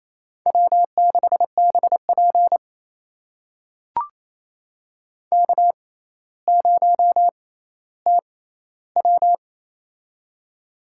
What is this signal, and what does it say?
Morse code, heard as a keyed tone, about 28 wpm, spaced as for 14 wpm, 700 Hz. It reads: W6BP K0TW
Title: USA Call Signs - 4 Characters - 6x Character Spacing - 28wpm